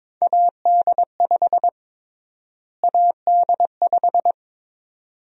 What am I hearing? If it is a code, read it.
Morse code, heard as a keyed tone, about 22 wpm, 700 Hz. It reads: AD5 AD5
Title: USA Call Sign Prefixes - Repeated 2x - 22wpm